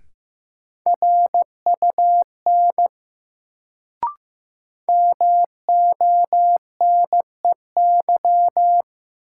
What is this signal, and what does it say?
Morse code, heard as a keyed tone, about 15 wpm, 700 Hz. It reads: RUN MONEY
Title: Sets of 4 Words - New Word Drill - 15wpm